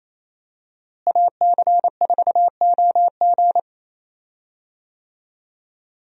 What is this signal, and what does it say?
Morse code, heard as a keyed tone, about 28 wpm, 700 Hz. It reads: AC4OG
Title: USA Call Signs - 5 Characters - Repeated 3x - 28wpm